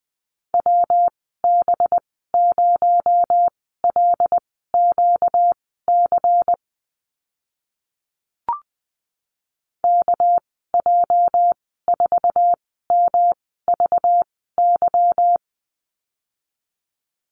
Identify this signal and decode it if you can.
Morse code, heard as a keyed tone, about 20 wpm, spaced as for 15 wpm, 700 Hz. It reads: WB0LQC KJ4MVY
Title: 238 Lesson 38g - Letter q - Call Signs 20wpm